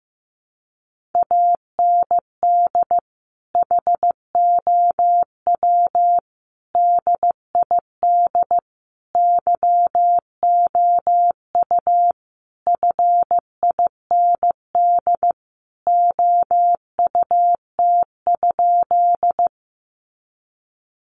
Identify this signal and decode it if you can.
Morse code, heard as a keyed tone, about 15 wpm, 700 Hz. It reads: AND HOW DID YOU FIND OUT?
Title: Sentences from Top 200 Words 15wpm